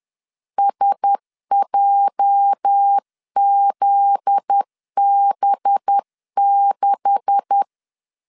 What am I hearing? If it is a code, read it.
Morse code, heard as a keyed tone, about 10 wpm, 800 Hz. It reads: SJZB6
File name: LEZIONI / 9 286 10wpm 2735